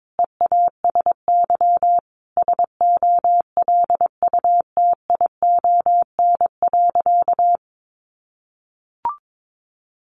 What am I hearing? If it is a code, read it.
Morse code, heard as a keyed tone, about 22 wpm, 700 Hz. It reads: EASY SOLUTION.
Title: Sentences from Top 1300 Words 22wpm